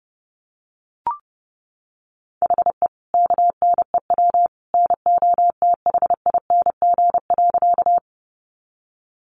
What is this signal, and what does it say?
Morse code, heard as a keyed tone, about 30 wpm, 700 Hz. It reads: HE KNEW NOTHING.
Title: Sentences from Top 500 Words 30wpm